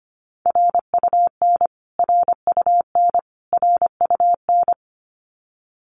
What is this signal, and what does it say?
Morse code, heard as a keyed tone, about 25 wpm, 700 Hz. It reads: RUN RUN RUN
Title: Top 500 Words Repeated 25wpm